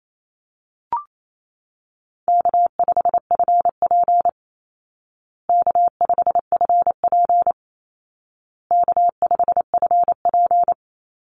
Morse code, heard as a keyed tone, about 28 wpm, 700 Hz. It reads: K5FP K5FP K5FP
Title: Easy Call Signs Repeated 3x - 28wpm